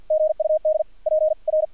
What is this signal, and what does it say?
Morse code, heard as a keyed tone, about 55 wpm, spaced as for 45 wpm, 620 Hz, with radio noise. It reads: GUD PR
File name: archived / CRSnet